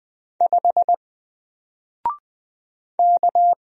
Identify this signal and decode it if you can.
Morse code, heard as a keyed tone, about 20 wpm, 700 Hz. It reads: 5 K
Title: Single Letter-Number 20wpm